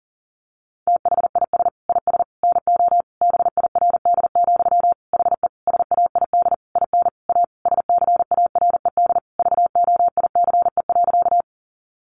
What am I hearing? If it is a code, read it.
Morse code, heard as a keyed tone, about 40 wpm, 700 Hz. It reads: THIS IS NO BIRD, HE SAID IN A SCARED VOICE.
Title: Sentences from Top 700 Words 40wpm